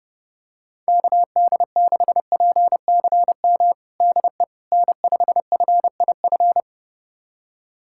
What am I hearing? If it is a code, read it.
Morse code, heard as a keyed tone, about 30 wpm, 700 Hz. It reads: KD6PCM DE N5FIF
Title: USA Callsign DE Callsign 30wpm